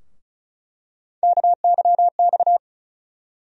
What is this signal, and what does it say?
Morse code, heard as a keyed tone, about 36 wpm, 700 Hz. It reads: KYX
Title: USA Call Sign Postfixes - 3 Letters 35wpm